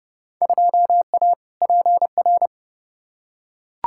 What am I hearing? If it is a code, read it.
Morse code, heard as a keyed tone, about 31 wpm, 700 Hz. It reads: 2A PR
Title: Field Day Exchanges Encore - 30wpm